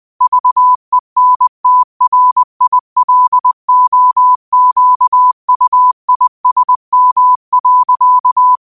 Morse code, heard as a keyed tone, about 20 wpm, 1000 Hz. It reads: VENTRILOQUISM.
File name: qotdfiles / 20wpm